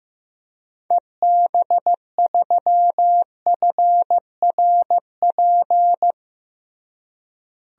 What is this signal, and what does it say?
Morse code, heard as a keyed tone, about 15 wpm, 700 Hz. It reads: EB3FRP
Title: DXCC-100 Call Signs - 6 Characters - Repeated 2x - 15wpm